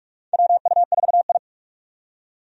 Morse code, heard as a keyed tone, about 46 wpm, 700 Hz. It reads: WU4I
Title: USA Call Signs - 4 Characters - 45wpm